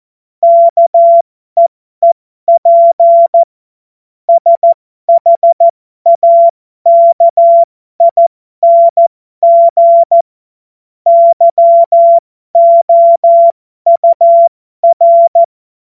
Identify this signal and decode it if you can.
Morse code, heard as a keyed tone, about 14 wpm, spaced as for 12 wpm, 680 Hz. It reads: KEEP SHAKING YOUR
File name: Morsekurs / Witze